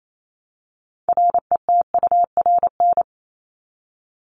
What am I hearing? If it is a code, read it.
Morse code, heard as a keyed tone, about 28 wpm, 700 Hz. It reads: RETURN
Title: ED Suffix - Top 100 Words with Base Words 28wpm